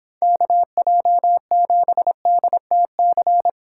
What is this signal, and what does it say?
Morse code, heard as a keyed tone, about 31 wpm, 700 Hz. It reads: KJ7DTC